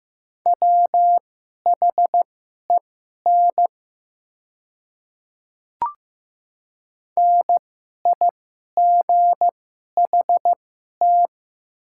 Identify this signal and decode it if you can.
Morse code, heard as a keyed tone, about 15 wpm, spaced as for 11 wpm, 700 Hz. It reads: WHEN NIGHT
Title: 2x Character Spacing - Top 300 Words - 15wpm